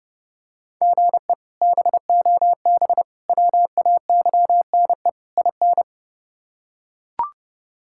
Morse code, heard as a keyed tone, about 30 wpm, 700 Hz. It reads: GE BOB WAYNE IN
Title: K1USN SST QSO Practice - 30wpm